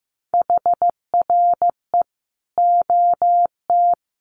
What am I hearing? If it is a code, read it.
Morse code, heard as a keyed tone, about 15 wpm, 700 Hz. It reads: HRE OT
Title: Common QSO Phrases 15wpm